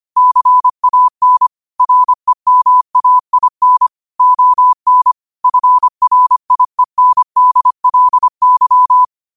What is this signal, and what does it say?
Morse code, heard as a keyed tone, about 25 wpm, 1000 Hz. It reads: CAN REMAIN ON FRIENDLY
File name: qotdfiles / 25wpm